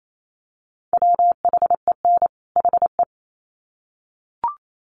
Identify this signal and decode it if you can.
Morse code, heard as a keyed tone, about 28 wpm, 700 Hz. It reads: WHEN HE
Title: Sets of 2 Words 28wpm